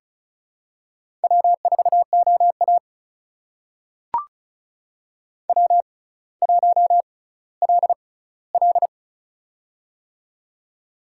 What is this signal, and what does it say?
Morse code, heard as a keyed tone, about 36 wpm, spaced as for 18 wpm, 700 Hz. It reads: W4OA W1LL
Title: USA Call Signs - 6x Character Spacing - 35wpm